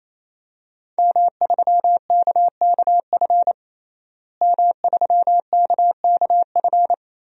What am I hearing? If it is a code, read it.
Morse code, heard as a keyed tone, about 27 wpm, 700 Hz. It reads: M3KKF M3KKF
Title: DXCC-100 Call Signs - 5 Characters - Repeated 3x - 28wpm